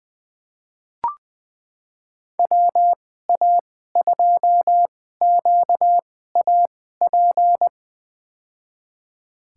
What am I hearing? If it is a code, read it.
Morse code, heard as a keyed tone, about 20 wpm, spaced as for 14 wpm, 700 Hz. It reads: WA2QAP
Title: USA Call Signs - 2x Character Spacing - 20wpm